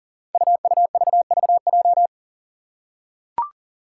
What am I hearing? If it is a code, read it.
Morse code, heard as a keyed tone, about 38 wpm, 690 Hz. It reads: UUVVJ
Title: Five Challenging Letter-Number Combinations 40wpm